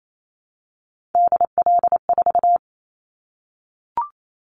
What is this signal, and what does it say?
Morse code, heard as a keyed tone, about 28 wpm, 700 Hz. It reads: DL4
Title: Callsign Prefixes for Top 25 DXCC - Repeated 3x - 28wpm